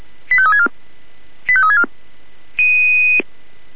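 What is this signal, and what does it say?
Five-tone selective call (ZVEI), heard as quick runs of five tones, tones of about 70 ms. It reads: 85364 85364
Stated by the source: ZVEI five-tone selective call 85364 received on an SDR.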